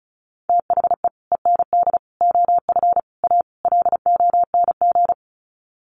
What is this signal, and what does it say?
Morse code, heard as a keyed tone, about 34 wpm, 700 Hz. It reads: THE END OF A LONG
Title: Sets of 5 Words Encore2 35wpm